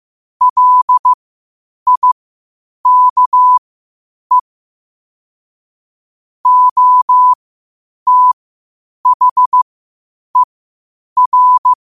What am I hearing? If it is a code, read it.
Morse code, heard as a keyed tone, about 15 wpm, spaced as for 8 wpm, 1000 Hz. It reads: LIKE OTHER
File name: qotdfiles / 8wpm